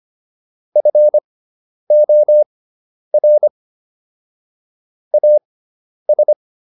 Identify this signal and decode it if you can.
Morse code, heard as a keyed tone, about 25 wpm, spaced as for 10 wpm, 600 Hz. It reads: FOR AS